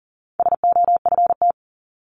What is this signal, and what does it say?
Morse code, heard as a keyed tone, about 40 wpm, 700 Hz. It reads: SOFT
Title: Top 1251-1500 Words 40wpm